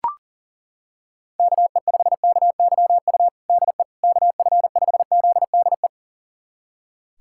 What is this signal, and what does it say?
Morse code, heard as a keyed tone, about 37 wpm, 700 Hz. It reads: KE5KYU DE KF5ZDE
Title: USA Callsign DE Callsign 40wpm